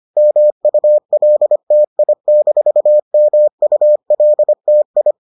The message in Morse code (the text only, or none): MULTI-MULTI